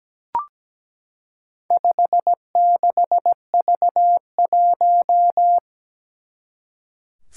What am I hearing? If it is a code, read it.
Morse code, heard as a keyed tone, about 17 wpm, 700 Hz. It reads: 56V1
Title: Four Challenging Letter-Number Combinations 17wpm